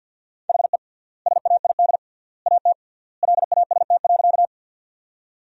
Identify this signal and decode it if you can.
Morse code, heard as a keyed tone, about 47 wpm, 700 Hz. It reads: HE SAID AT LAST.
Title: 3x Word-Spacing - Sentences from Top 300 Words - 50wpm